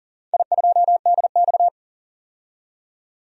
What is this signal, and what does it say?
Morse code, heard as a keyed tone, about 37 wpm, 700 Hz. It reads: I2DX
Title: DXCC-100 Call Signs - 4 Characters - 40wpm